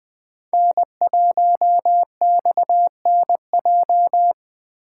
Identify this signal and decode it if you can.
Morse code, heard as a keyed tone, about 20 wpm, 700 Hz. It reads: N1XNJ
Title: USA Call Signs - 5 Characters - 2x Character Spacing - 20wpm